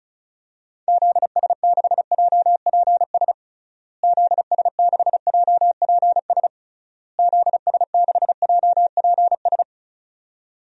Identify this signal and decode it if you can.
Morse code, heard as a keyed tone, about 36 wpm, 700 Hz. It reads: ZS6JPS ZS6JPS ZS6JPS
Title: DXCC-100 Call Signs - 6 Characters - Repeated 3x - 35wpm